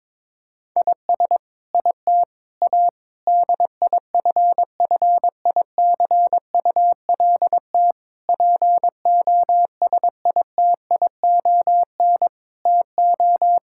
Morse code, heard as a keyed tone, about 22 wpm, 700 Hz. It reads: IS IT A DIFFICULT POSITION TO
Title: Sentences from Top 1000 Words - Review Part 5 - 22wpm